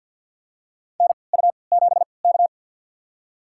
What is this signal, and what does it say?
Morse code, heard as a keyed tone, about 51 wpm, spaced as for 28 wpm, 700 Hz. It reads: NU7K
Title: USA Call Signs - 3x Character Spacing - 50wpm